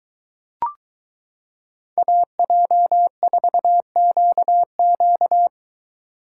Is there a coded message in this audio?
Yes: Morse code from 2.0 to 5.5 s, heard as a keyed tone, about 22 wpm, 700 Hz. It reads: AJ4QQ